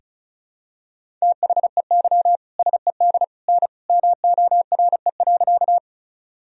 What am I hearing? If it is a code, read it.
Morse code, heard as a keyed tone, about 34 wpm, 700 Hz. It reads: THEY SED N MORE.
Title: Sentences from Top 100 Words and Abbreviations 35wpm